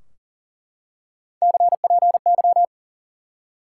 Morse code, heard as a keyed tone, about 39 wpm, 700 Hz. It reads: CPY
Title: Top 90 QSO Elements 40wpm